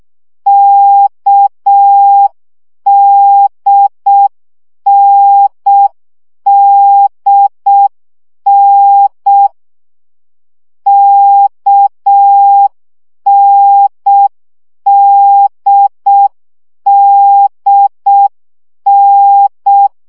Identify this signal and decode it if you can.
Morse code, heard as a keyed tone, about 6 wpm, 800 Hz. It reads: KDNDN KNDDN
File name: CW gyakorlas n d k 5 baud